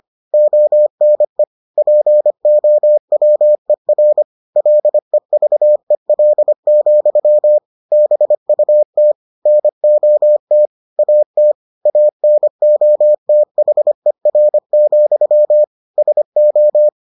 Morse code, heard as a keyed tone, about 25 wpm, 600 Hz. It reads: ONE POWER LEVEL, BUT NOT AT ANOTHER, SO